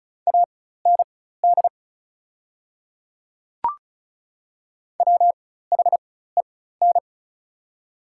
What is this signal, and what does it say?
Morse code, heard as a keyed tone, about 36 wpm, spaced as for 16 wpm, 700 Hz. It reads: AND WHEN
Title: 4x Character Spacing - Top 300 Words - 35wpm